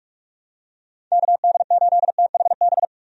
Morse code, heard as a keyed tone, about 45 wpm, 700 Hz. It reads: KD8THB
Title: USA Call Signs - 6 Characters - 45wpm